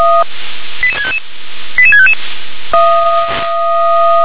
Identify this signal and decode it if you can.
Five-tone selective call (ZVEI), heard as quick runs of five tones, tones of about 70 ms. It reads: 88655 88655
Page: SDR/ZVEI